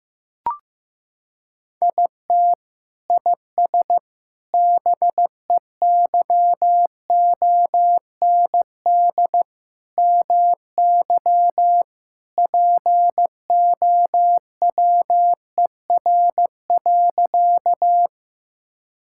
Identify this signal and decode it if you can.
Morse code, heard as a keyed tone, about 15 wpm, 700 Hz. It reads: IT IS BEYOND MY POWER.
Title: Sentences from Top 1000 Words - Review Part 6 - 15wpm